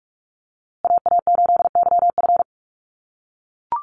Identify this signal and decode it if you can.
Morse code, heard as a keyed tone, about 44 wpm, 690 Hz. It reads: AA8YF
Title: USA Call Signs Repeated 3x 45wpm